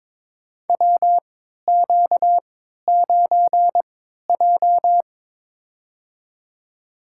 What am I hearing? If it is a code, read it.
Morse code, heard as a keyed tone, about 22 wpm, spaced as for 12 wpm, 700 Hz. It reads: WQ9J
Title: USA Call Signs - 4 Characters - 3x Character Spacing - 22wpm